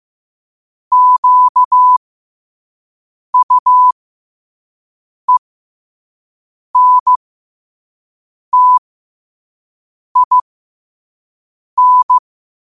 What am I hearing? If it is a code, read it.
Morse code, heard as a keyed tone, about 15 wpm, spaced as for 5 wpm, 1000 Hz. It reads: QUENTIN
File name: qotdfiles / 5wpm